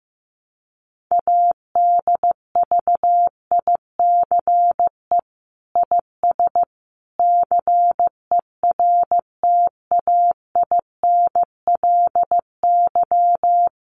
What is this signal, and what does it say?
Morse code, heard as a keyed tone, about 15 wpm, 700 Hz. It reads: ADVICE IS CERTAINLY
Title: Sentences from Top 1100 Words 15wpm